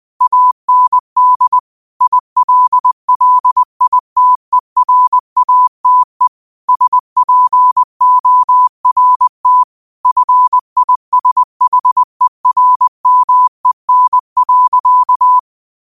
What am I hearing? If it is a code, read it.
Morse code, heard as a keyed tone, about 20 wpm, 1000 Hz. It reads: AND ILLITERATE SPORT FISHERMEN.